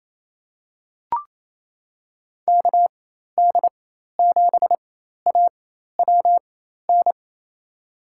Morse code, heard as a keyed tone, about 28 wpm, spaced as for 13 wpm, 700 Hz. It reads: KD7AWN
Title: USA Call Signs - 4x Character Spacing - 28wpm